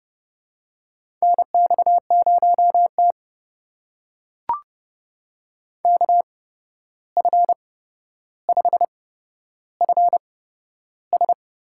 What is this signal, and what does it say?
Morse code, heard as a keyed tone, about 29 wpm, spaced as for 12 wpm, 700 Hz. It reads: NX0T KF5FS
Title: USA Call Signs - 8x Character Spacing - 30wpm